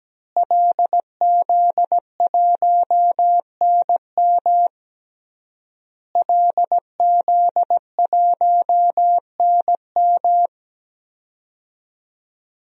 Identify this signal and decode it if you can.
Morse code, heard as a keyed tone, about 17 wpm, 700 Hz. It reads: LZ1NM LZ1NM
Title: DXCC-100 Call Signs - 5 Characters - Repeated 3x - 17wpm